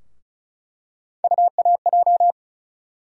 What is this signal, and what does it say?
Morse code, heard as a keyed tone, about 36 wpm, 700 Hz. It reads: UAJ
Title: USA Call Sign Postfixes - 3 Letters 35wpm